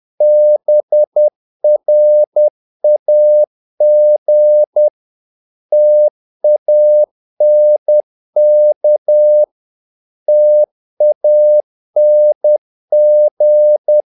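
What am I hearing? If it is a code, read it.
Morse code, heard as a keyed tone, about 10 wpm, 600 Hz. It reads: BRAG TANK TANG
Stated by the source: Morse practice of short Danish words at 10 wpm, from CW med Gnister.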